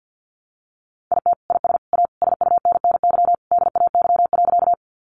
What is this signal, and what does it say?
Morse code, heard as a keyed tone, about 50 wpm, 700 Hz. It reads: IT IS A SUNNY DAY.
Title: Additional Easy Phrases 50wpm